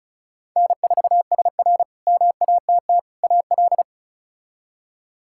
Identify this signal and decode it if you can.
Morse code, heard as a keyed tone, about 34 wpm, 700 Hz. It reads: N4SR MATT AL
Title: K1USN SST QSO Practice - 35wpm